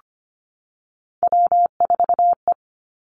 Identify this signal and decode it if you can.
Morse code, heard as a keyed tone, about 25 wpm, 700 Hz. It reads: W4E
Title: Three Challenging Letter-Number Combinations 25wpm